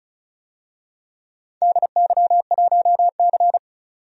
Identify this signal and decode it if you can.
Morse code, heard as a keyed tone, about 36 wpm, 700 Hz. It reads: DY1C
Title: DXCC-100 Call Signs - 4 Characters - 35wpm